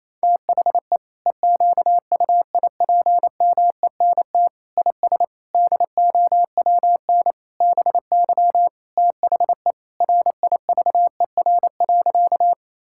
Morse code, heard as a keyed tone, about 28 wpm, 700 Hz. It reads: THE EQUIPMENT IS DOWN BY THE RIVER.
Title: Sentences from Top 1000 Words - Review Part 5 - 28wpm